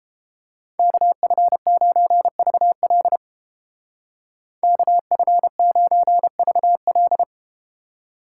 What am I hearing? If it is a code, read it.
Morse code, heard as a keyed tone, about 30 wpm, 700 Hz. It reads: KF9VL KF9VL